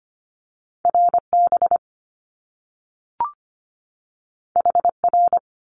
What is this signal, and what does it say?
Morse code, heard as a keyed tone, about 25 wpm, 700 Hz. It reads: RB HR
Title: Top Two Letter Combinations in Words 25wpm